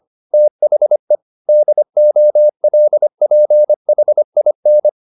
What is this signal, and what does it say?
Morse code, heard as a keyed tone, about 25 wpm, 600 Hz. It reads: THE DOLPHIN